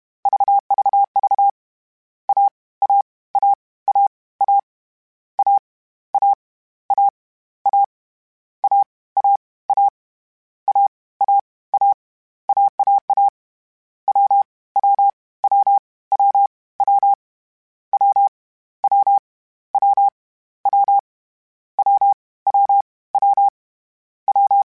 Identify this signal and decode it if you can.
Morse code, heard as a keyed tone, about 31 wpm, 800 Hz. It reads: VVV A A A A A A A A A A A A A A A AAA W W W W W W W W W W W W W
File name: Difficult Combos / Difficult Combos-mp3s / AWJ1 @30